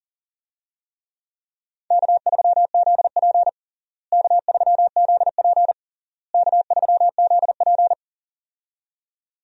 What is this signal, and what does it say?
Morse code, heard as a keyed tone, about 40 wpm, 700 Hz. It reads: K3ZP K3ZP K3ZP
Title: USA Call Signs - 4 Characters - Repeated 3x - 40wpm